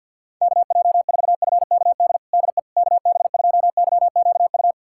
Morse code, heard as a keyed tone, about 47 wpm, 700 Hz. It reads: KJ4FKD DE KB2YQU
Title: USA Callsign DE Callsign 50wpm